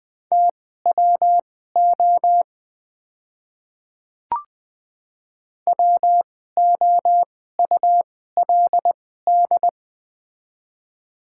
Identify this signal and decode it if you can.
Morse code, heard as a keyed tone, about 21 wpm, spaced as for 15 wpm, 700 Hz. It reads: TWO WOULD
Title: 2x Character Spacing - Top 100 Words - 20wpm